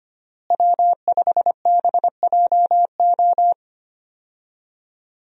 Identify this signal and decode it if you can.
Morse code, heard as a keyed tone, about 25 wpm, 700 Hz. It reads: W5BJO
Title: USA Call Signs - 5 Characters - Repeated 3x - 25wpm